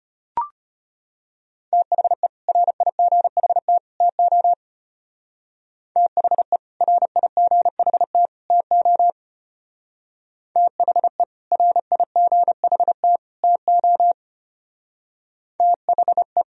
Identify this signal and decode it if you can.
Morse code, heard as a keyed tone, about 31 wpm, 700 Hz. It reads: THE RIGHT TO THE RIGHT TO THE RIGHT TO THE